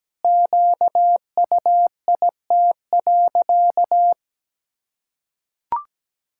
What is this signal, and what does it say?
Morse code, heard as a keyed tone, about 17 wpm, 700 Hz. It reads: QUIT.
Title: Pangram Challenge v2 - 17wpm